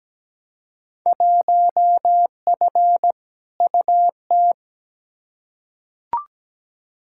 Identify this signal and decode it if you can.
Morse code, heard as a keyed tone, about 17 wpm, 700 Hz. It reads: 1F UT
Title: Field Day Exchanges Encore - 17wpm